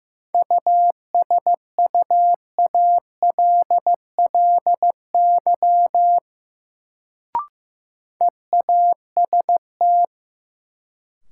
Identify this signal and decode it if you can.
Morse code, heard as a keyed tone, about 15 wpm, 700 Hz. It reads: USUALLY EAST
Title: Sentences from Top 1000 Words - Review Part 3 - New Word Drill - 15wpm